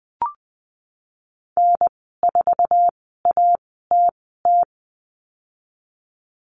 Morse code, heard as a keyed tone, about 20 wpm, spaced as for 15 wpm, 700 Hz. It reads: N4ATT
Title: 069 Lesson 12g - Letter d - Call Signs 20wpm